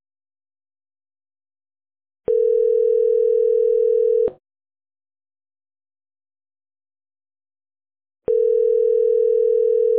Ringback tone (North America), 440 + 480 Hz.